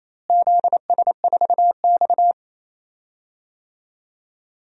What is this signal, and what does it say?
Morse code, heard as a keyed tone, about 28 wpm, 700 Hz. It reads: ZS4X
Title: DXCC-100 Call Signs - 4 Characters - Repeated 3x - 28wpm